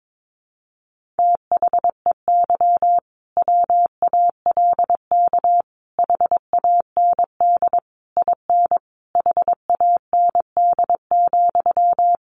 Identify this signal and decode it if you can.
Morse code, heard as a keyed tone, about 22 wpm, 700 Hz. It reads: THEY WALK HAND IN HAND,
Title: Sentences from Top 1400 Words 22wpm